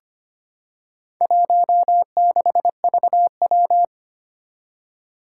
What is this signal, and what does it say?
Morse code, heard as a keyed tone, about 25 wpm, 700 Hz. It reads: 16VW
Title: Four Challenging Letter-Number Combinations 25wpm